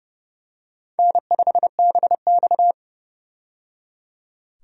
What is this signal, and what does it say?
Morse code, heard as a keyed tone, about 29 wpm, 700 Hz. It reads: N5BX
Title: USA Call Signs - 4 Characters - 30wpm